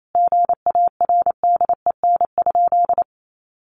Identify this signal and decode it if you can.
Morse code, heard as a keyed tone, about 28 wpm, 700 Hz. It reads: GARDEN?
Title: Sentences from Top 700 Words 28wpm